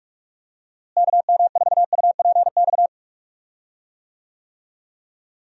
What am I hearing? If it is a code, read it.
Morse code, heard as a keyed tone, about 42 wpm, 700 Hz. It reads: KM4UPX
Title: USA Call Signs - 6 Characters - Repeated 2x - 45wpm